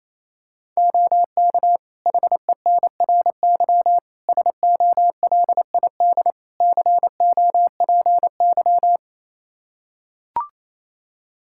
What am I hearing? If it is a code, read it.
Morse code, heard as a keyed tone, about 28 wpm, 700 Hz. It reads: OK HENRY SOLID COPY
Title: Practice QSO - Exchange 3 - More Info - 28wpm